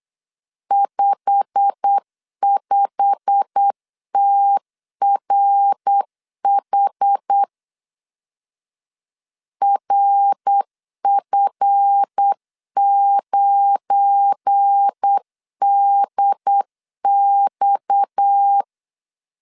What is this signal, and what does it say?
Morse code, heard as a keyed tone, about 8 wpm, 800 Hz. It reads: 55TRH RF9DX
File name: LEZIONI / 8 YJ45 / 8 YJ45 8wpm 3214